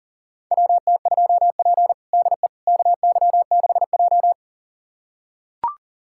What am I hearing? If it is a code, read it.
Morse code, heard as a keyed tone, about 40 wpm, 700 Hz. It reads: WT2P DE KY6J
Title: USA Callsign DE Callsign - 4 Characters - 40wpm